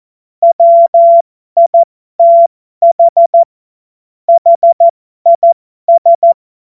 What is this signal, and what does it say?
Morse code, heard as a keyed tone, about 14 wpm, spaced as for 12 wpm, 680 Hz. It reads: WITH HIS
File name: Morsekurs / Witze